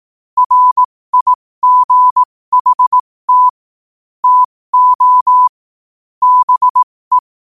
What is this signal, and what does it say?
Morse code, heard as a keyed tone, about 18 wpm, spaced as for 15 wpm, 1000 Hz. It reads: RIGHT TO BE
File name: qotdfiles / 15wpm